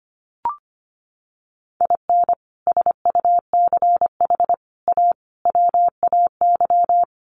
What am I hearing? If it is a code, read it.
Morse code, heard as a keyed tone, about 25 wpm, 700 Hz. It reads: IN SUCH A WAY
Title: Sets of 5 Words 25wpm